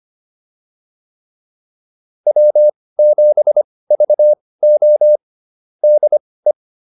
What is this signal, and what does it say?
Morse code, heard as a keyed tone, about 25 wpm, spaced as for 18 wpm, 600 Hz. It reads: W7VO DE
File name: QSO 105 18